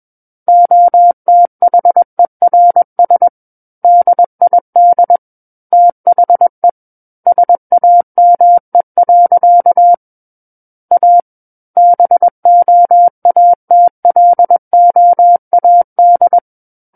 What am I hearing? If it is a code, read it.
Morse code, heard as a keyed tone, about 21 wpm, 700 Hz. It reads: OTHERS DID THE SAME. A BOATLOAD
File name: Book One / 12-Chapter 12 - 21 WPM